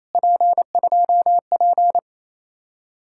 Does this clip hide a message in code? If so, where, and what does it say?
Yes: Morse code from 0.2 to 2.0 s, heard as a keyed tone, about 36 wpm, 700 Hz. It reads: P2P